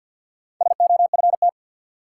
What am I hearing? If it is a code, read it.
Morse code, heard as a keyed tone, about 53 wpm, 700 Hz. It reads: SOFT
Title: Sentences from Top 1500 Words - New Word Drill - 50wpm